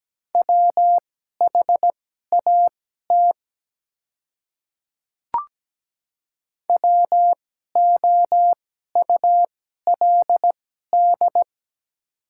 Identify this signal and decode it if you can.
Morse code, heard as a keyed tone, about 17 wpm, spaced as for 12 wpm, 700 Hz. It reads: WHAT WOULD
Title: 2x Character Spacing - Top 100 Words - 17wpm